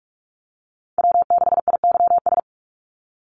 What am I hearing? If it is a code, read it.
Morse code, heard as a keyed tone, about 44 wpm, 700 Hz. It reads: W6IYS
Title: Call Signs - 45wpm